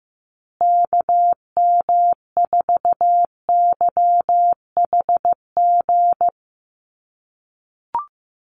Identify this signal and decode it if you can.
Morse code, heard as a keyed tone, about 14 wpm, 700 Hz. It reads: KM4YHG
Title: USA Call Signs Repeated 2x 15wpm